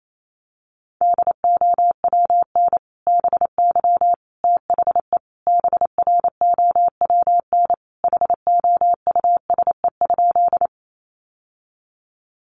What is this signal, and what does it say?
Morse code, heard as a keyed tone, about 28 wpm, 700 Hz. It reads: DOWN BY THE BROWN HOUSE?
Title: Sentences from Top 1200 Words 28wpm